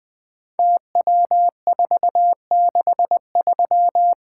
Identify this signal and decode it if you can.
Morse code, heard as a keyed tone, about 20 wpm, 700 Hz. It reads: TW463
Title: Code Groups - 5 Letter:Numbers - 20wpm